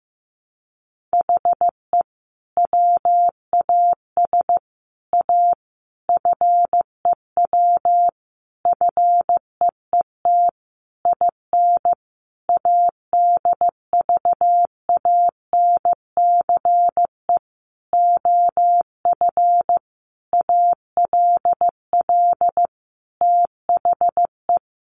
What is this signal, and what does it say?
Morse code, heard as a keyed tone, about 15 wpm, 700 Hz. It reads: HE WAS A FEW FEET IN ADVANCE OF ALL THE
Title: Sentences from Top 1400 Words 15wpm